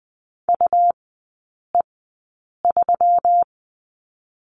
Morse code, heard as a keyed tone, about 20 wpm, spaced as for 8 wpm, 700 Hz. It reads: UE3
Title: 166 Lesson 27e - Number 7 - Three Characters 20wpm